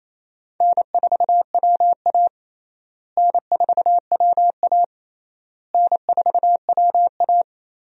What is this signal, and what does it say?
Morse code, heard as a keyed tone, about 28 wpm, 700 Hz. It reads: N4WA N4WA N4WA
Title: USA Call Signs - 4 Characters - Repeated 3x - 28wpm